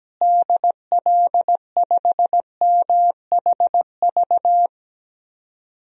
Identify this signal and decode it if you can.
Morse code, heard as a keyed tone, about 17 wpm, 700 Hz. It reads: DL5MHV
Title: DXCC-100 Call Signs - 6 Characters - Repeated 2x - 17wpm